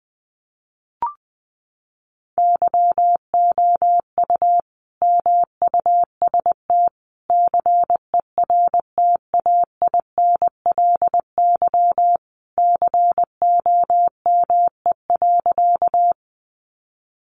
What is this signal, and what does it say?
Morse code, heard as a keyed tone, about 20 wpm, 700 Hz. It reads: YOU MUST CERTAINLY COME.
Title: Sentences from Top 900 Words 20wpm